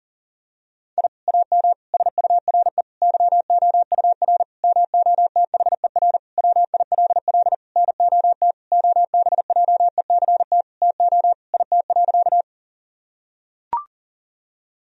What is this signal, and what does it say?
Morse code, heard as a keyed tone, about 37 wpm, 700 Hz. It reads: I AM SURE YOUR MOTHER WILL NOT OBJECT TO IT.
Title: Sentences from Top 1000 Words - Review Part 3 - 40wpm